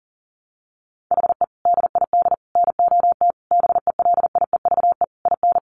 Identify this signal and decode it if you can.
Morse code, heard as a keyed tone, about 37 wpm, 700 Hz. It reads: HE DID NOT BELIEVE IN
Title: Sentences from Top 1300 Words 40wpm